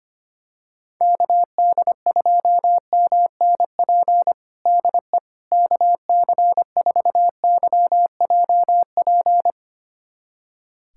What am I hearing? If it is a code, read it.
Morse code, heard as a keyed tone, about 25 wpm, 700 Hz. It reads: KD2MNP DE KC4YJP
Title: USA Callsign DE Callsign 25wpm